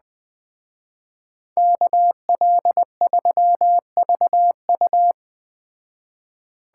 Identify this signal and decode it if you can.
Morse code, heard as a keyed tone, about 20 wpm, 700 Hz. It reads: KL3VU
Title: Speed Racing - USA 2x2 Call Signs 17wpm